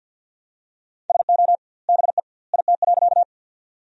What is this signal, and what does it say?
Morse code, heard as a keyed tone, about 50 wpm, 700 Hz. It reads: SO BE IT.
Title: 2x Word-Spacing - Sentences from Top 100 Words - 50wpm